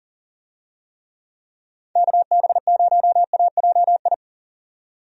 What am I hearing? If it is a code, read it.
Morse code, heard as a keyed tone, about 40 wpm, 700 Hz. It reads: KB0AJI
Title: USA Call Signs - 6 Characters - 40wpm